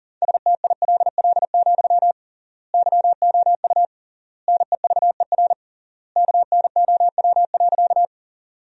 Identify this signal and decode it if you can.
Morse code, heard as a keyed tone, about 40 wpm, 700 Hz. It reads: STILL, YOU NEVER KNOW.
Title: 3x Word-Spacing - Sentences from Top 300 Words - 40wpm